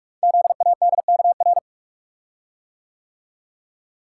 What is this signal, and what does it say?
Morse code, heard as a keyed tone, about 44 wpm, 700 Hz. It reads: ZADKR
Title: Code Groups - 5 Letters - 45wpm